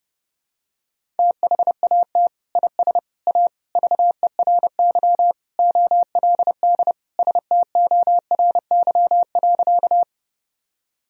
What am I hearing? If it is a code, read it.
Morse code, heard as a keyed tone, about 32 wpm, 700 Hz. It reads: THAT IS A VERY OLD STORY.
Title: Sentences from Top 1000 Words - Review Part 1 - 30wpm